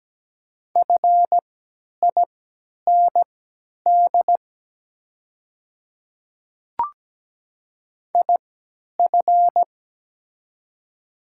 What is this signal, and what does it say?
Morse code, heard as a keyed tone, about 17 wpm, spaced as for 10 wpm, 700 Hz. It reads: FIND IF
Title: 3x Character Spacing - Top 100 Words - 17wpm